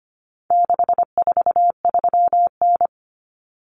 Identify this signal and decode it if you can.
Morse code, heard as a keyed tone, about 25 wpm, 700 Hz. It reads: 643N
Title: Common Cut Numbers - 4-Digit 25wpm